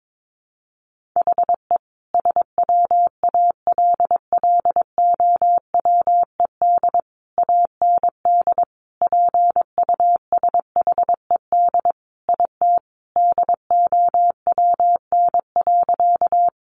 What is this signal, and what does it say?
Morse code, heard as a keyed tone, about 22 wpm, 700 Hz. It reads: HE SWALLOWED AND PUSHED IT DOWN.
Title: A Sound Of Thunder 22wpm